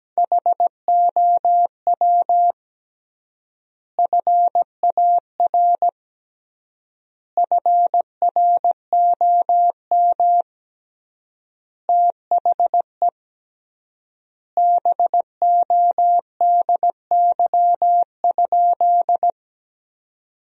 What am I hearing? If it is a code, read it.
Morse code, heard as a keyed tone, about 17 wpm, 700 Hz. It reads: HOW FAR FROM THE BODY?
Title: 3x Word-Spacing - Sentences from Top 400 Words - 17wpm